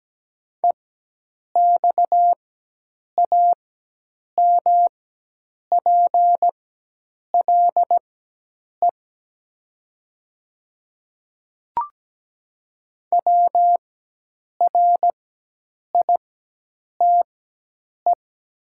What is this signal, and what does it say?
Morse code, heard as a keyed tone, about 17 wpm, spaced as for 8 wpm, 700 Hz. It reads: EXAMPLE WRITE
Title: 4x Character Spacing - Top 300 Words - 17wpm